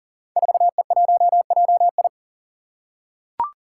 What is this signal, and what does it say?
Morse code, heard as a keyed tone, about 41 wpm, 700 Hz. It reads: 4E1JI
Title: Five Challenging Letter-Number Combinations 40wpm